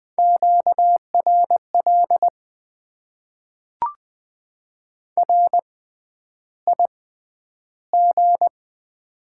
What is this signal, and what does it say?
Morse code, heard as a keyed tone, about 20 wpm, spaced as for 10 wpm, 700 Hz. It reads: QRL RIG
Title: Top 20 QSO Elements - 6x Character Spacing - 20wpm